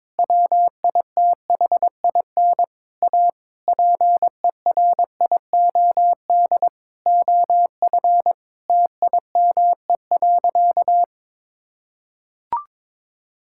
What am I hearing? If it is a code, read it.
Morse code, heard as a keyed tone, about 21 wpm, 700 Hz. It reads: WITHIN A PERIOD OF TIME.
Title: Sentences from Top 1000 Words - Review Part 5 - 22wpm